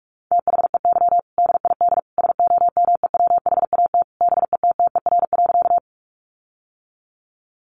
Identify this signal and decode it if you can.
Morse code, heard as a keyed tone, about 44 wpm, 700 Hz. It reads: THEY DID SOMEWHAT BETTER.
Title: Sentences from Top 1300 Words 45wpm